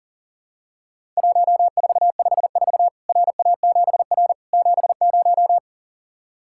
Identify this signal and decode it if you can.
Morse code, heard as a keyed tone, about 40 wpm, 700 Hz. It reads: 1454 RA7R 70
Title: Worked All Europe DX Contest - QTC Exchange - 40wpm